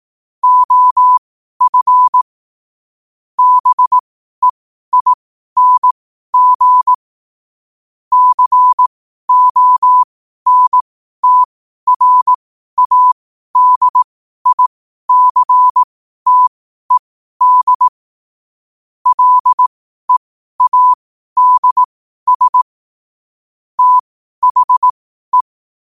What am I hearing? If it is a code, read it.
Morse code, heard as a keyed tone, about 18 wpm, spaced as for 12 wpm, 1000 Hz. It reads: OF BEING CONTRADICTED LEADS THE
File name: qotdfiles / 12wpm